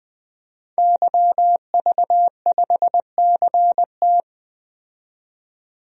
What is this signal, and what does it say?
Morse code, heard as a keyed tone, about 21 wpm, 700 Hz. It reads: YV5CT
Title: DXCC-100 Call Signs - 5 Characters - 20wpm